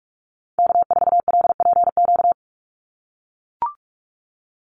Morse code, heard as a keyed tone, about 44 wpm, 700 Hz. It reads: K4LPQ
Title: NA SOTA Activator Callsigns 45wpm